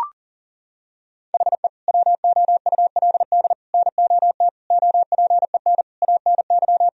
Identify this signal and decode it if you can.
Morse code, heard as a keyed tone, about 40 wpm, 700 Hz. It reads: HE WOULD NOT OPEN ANY
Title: Sentences from Top 1000 Words - Review Part 1 - 40wpm